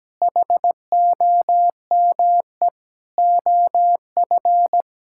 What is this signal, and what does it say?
Morse code, heard as a keyed tone, about 17 wpm, 700 Hz. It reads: HOME OF
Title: Sentences from Top 1000 Words - Review Part 5 - 17wpm